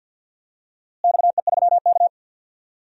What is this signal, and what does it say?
Morse code, heard as a keyed tone, about 46 wpm, 700 Hz. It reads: XE3K
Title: DXCC-100 Call Signs - 4 Characters - Repeated 2x - 50wpm